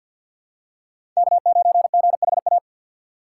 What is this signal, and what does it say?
Morse code, heard as a keyed tone, about 51 wpm, 700 Hz. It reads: K9GHA
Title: USA Call Signs - 5 Characters - 2x Character Spacing - 50wpm